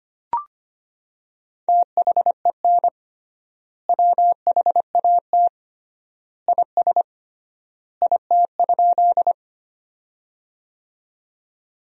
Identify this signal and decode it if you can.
Morse code, heard as a keyed tone, about 25 wpm, 700 Hz. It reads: THEN WHAT IS IT?
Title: 3x Word-Spacing - Sentences from Top 100 Words - 25wpm